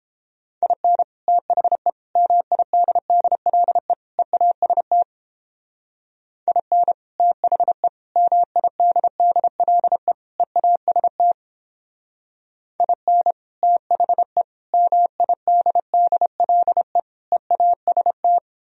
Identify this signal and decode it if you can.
Morse code, heard as a keyed tone, about 29 wpm, 700 Hz. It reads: IN THE MIDDLE EAST IN THE MIDDLE EAST IN THE MIDDLE EAST